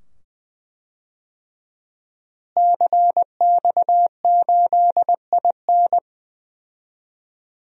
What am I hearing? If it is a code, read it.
Morse code, heard as a keyed tone, about 21 wpm, 700 Hz. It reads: CX8IN
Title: DXCC-100 Call Signs - 5 Characters - Repeated 2x - 20wpm